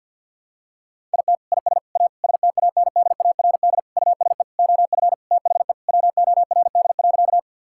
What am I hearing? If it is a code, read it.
Morse code, heard as a keyed tone, about 50 wpm, 700 Hz. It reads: IT IS A STANDARD USE OF THE WORD.
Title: Sentences from Top 1000 Words - Review Part 6 - 50wpm